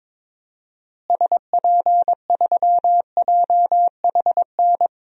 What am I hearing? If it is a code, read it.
Morse code, heard as a keyed tone, about 22 wpm, 700 Hz. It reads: SP3JHN
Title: DXCC-100 Call Signs - 6 Characters - Repeated 2x - 22wpm